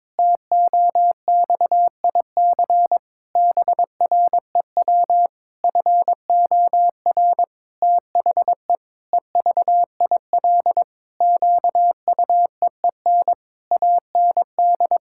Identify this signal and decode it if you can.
Morse code, heard as a keyed tone, about 22 wpm, 700 Hz. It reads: TOXIC BREW FOR THE EVIL QUEEN AND
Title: Pangram Challenge v2 - 22wpm